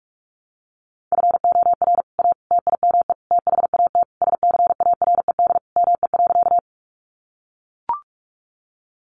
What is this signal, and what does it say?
Morse code, heard as a keyed tone, about 44 wpm, 700 Hz. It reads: FOR A TIME THAT SCARED ME.
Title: Sentences from Top 700 Words 45wpm